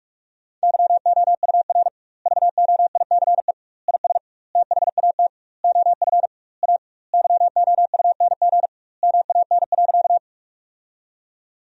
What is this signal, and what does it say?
Morse code, heard as a keyed tone, about 43 wpm, 700 Hz. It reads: YOUR VOICE IS THAT OF A YOUNG MAN.
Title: 2x Word-Spacing - Sentences from Top 500 Words - 45wpm